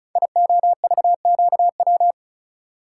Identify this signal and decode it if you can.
Morse code, heard as a keyed tone, about 36 wpm, 700 Hz. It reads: IOVQW
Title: Code Groups - 5 Letter:Numbers - 35wpm